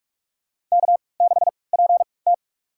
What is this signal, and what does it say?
Morse code, heard as a keyed tone, about 46 wpm, spaced as for 25 wpm, 700 Hz. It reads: K6PT
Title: USA Call Signs - 4 Characters - 3x Character Spacing - 45wpm